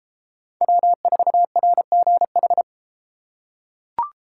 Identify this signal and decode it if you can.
Morse code, heard as a keyed tone, about 29 wpm, 700 Hz. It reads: W4RGH